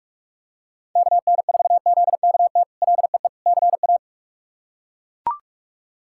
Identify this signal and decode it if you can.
Morse code, heard as a keyed tone, about 41 wpm, 700 Hz. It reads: KN4ZKT LEE CA
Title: K1USN SST QSO Practice - 45wpm